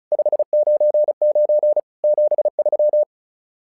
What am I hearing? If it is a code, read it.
Morse code, heard as a keyed tone, about 36 wpm, 600 Hz. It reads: 599 73
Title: SOTA QSO Practice 35wpm